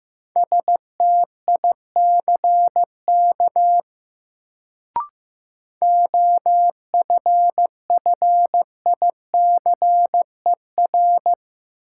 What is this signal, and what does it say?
Morse code, heard as a keyed tone, about 15 wpm, 700 Hz. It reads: STICK OFFICER
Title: Sentences from Top 1200 Words - New Word Drill Alt - 15wpm